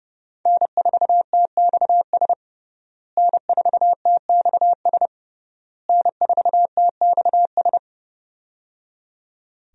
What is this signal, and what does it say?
Morse code, heard as a keyed tone, about 30 wpm, 700 Hz. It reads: N4TXS N4TXS N4TXS
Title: USA Call Signs - 5 Characters - Repeated 3x - 30wpm